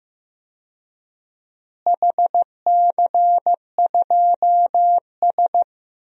Morse code, heard as a keyed tone, about 15 wpm, 700 Hz. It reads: HC2S
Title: DXCC-100 Call Signs - 4 Characters - Repeated 2x - 15wpm